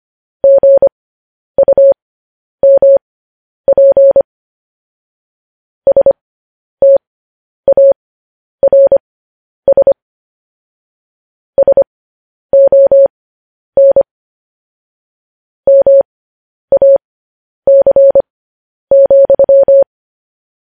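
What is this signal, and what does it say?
Morse code, heard as a keyed tone, about 25 wpm, spaced as for 10 wpm, 550 Hz. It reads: GUMP STARS SON MAC,
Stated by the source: Morse code news headlines at 10.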